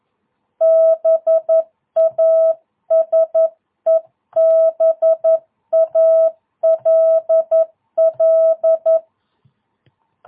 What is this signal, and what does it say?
Morse code, heard as a keyed tone, about 11 wpm, 660 Hz. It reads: BASEBALL
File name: jh8cbh / 2022cw